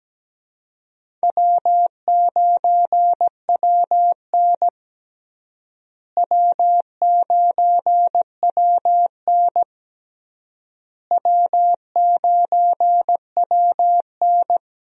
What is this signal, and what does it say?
Morse code, heard as a keyed tone, about 17 wpm, 700 Hz. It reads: W9WN W9WN W9WN
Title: USA Call Signs - 4 Characters - Repeated 3x - 17wpm